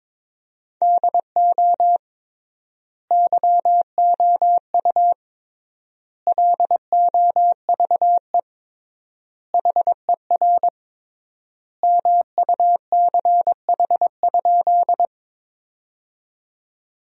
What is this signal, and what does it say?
Morse code, heard as a keyed tone, about 22 wpm, 700 Hz. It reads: DO YOU LOVE HER MUCH?
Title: 3x Word-Spacing - Sentences from Top 400 Words - 22wpm